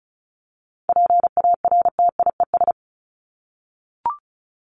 Morse code, heard as a keyed tone, about 34 wpm, 700 Hz. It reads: PARTIES
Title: ES:S Suffix - Top 100 Words with Base Words 35wpm